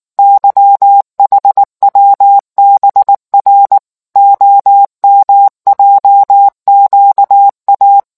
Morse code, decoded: YHWBR OMJQA

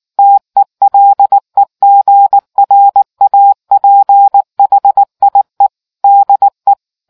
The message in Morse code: TELEGRAPHIE DE

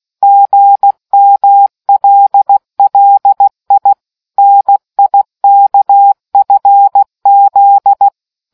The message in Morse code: GMLLI NIKFZ

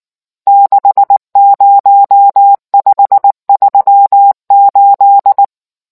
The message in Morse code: 60538

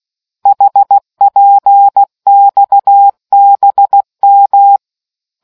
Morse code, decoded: HPXBM

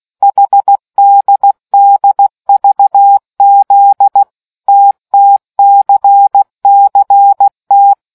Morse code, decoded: HDDVZ TTCCT

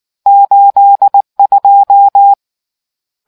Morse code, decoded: 82

